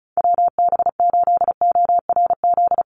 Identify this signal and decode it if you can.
Morse code, heard as a keyed tone, about 34 wpm, 700 Hz. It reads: WB8ORZ
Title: USA Call Signs 35wpm